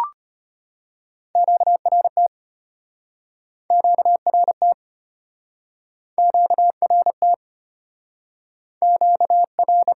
Morse code, heard as a keyed tone, about 33 wpm, 700 Hz. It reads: QRT QRT QRT QR